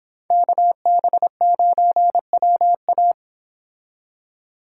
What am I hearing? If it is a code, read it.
Morse code, heard as a keyed tone, about 26 wpm, 700 Hz. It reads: KB9WA